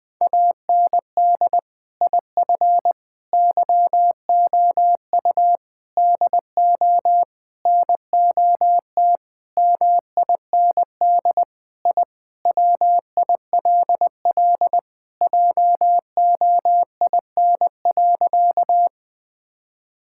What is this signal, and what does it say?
Morse code, heard as a keyed tone, about 20 wpm, 700 Hz. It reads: AND IF YOU DO NOT MIND I WILL JOIN.
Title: Sentences with 5-Word Phrases - Spoken Beginning - 20wpm